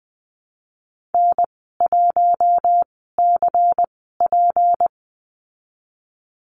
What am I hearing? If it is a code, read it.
Morse code, heard as a keyed tone, about 20 wpm, spaced as for 15 wpm, 700 Hz. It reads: N1CP
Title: 154 Lesson 25g - Letter p - Call Signs 20wpm